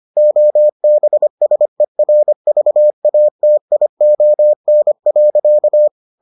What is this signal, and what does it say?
Morse code, heard as a keyed tone, about 25 wpm, 600 Hz. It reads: OBSERVATION.